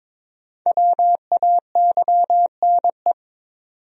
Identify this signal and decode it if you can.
Morse code, heard as a keyed tone, about 22 wpm, 700 Hz. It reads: WAYNE
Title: USA Common Names - Max 5 Letters - 22wpm